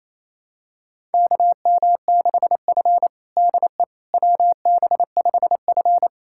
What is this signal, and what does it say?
Morse code, heard as a keyed tone, about 28 wpm, 700 Hz. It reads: KM6F DE WB5F
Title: USA Callsign DE Callsign - 4 Characters - 28wpm